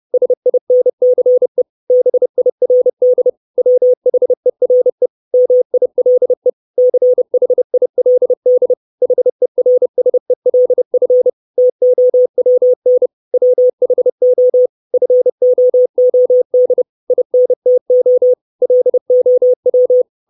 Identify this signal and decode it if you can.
Morse code, heard as a keyed tone, about 30 wpm, 500 Hz. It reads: SINCE BIRD WHERE MILE CHILD HERSELF TOWN WHO FOOD INTO LOW
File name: 632 Words 30WPM